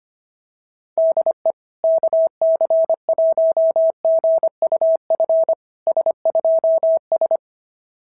Morse code, heard as a keyed tone, about 23 wpm, 650 Hz. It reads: DE KC1GUF S2S
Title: SOTA QSO Practice 25wpm